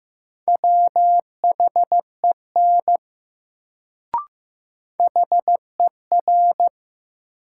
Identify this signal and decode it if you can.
Morse code, heard as a keyed tone, about 15 wpm, 700 Hz. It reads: WHEN HER
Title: Sets of 2 Words - New Word Drill - 15wpm